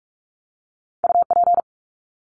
Morse code, heard as a keyed tone, about 44 wpm, 700 Hz. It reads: UP
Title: Top 300 Words 45wpm